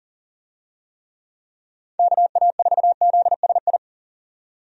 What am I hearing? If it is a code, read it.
Morse code, heard as a keyed tone, about 41 wpm, 700 Hz. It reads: KA4ZSI
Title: USA Call Signs - 6 Characters - 40wpm